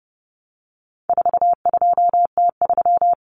Morse code, heard as a keyed tone, about 29 wpm, 700 Hz. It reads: 42T3
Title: Common Cut Numbers - 4-Digit 30wpm